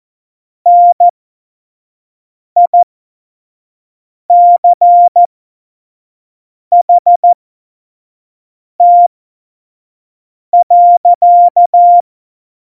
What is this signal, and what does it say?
Morse code, heard as a keyed tone, about 14 wpm, spaced as for 5 wpm, 700 Hz. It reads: NICHT.
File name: Morsekurs / Witze / Witz 18 August 2020, 25 F70